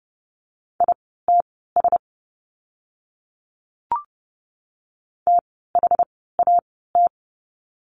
Morse code, heard as a keyed tone, about 30 wpm, spaced as for 17 wpm, 700 Hz. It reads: ITS THAT
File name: course / 30:10wpm / 237 Lesson 38f - Letter q - Words 30wpm